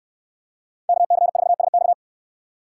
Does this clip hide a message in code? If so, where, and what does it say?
Yes: Morse code from 0.9 to 1.9 s, heard as a keyed tone, about 68 wpm, 700 Hz. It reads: KC4SX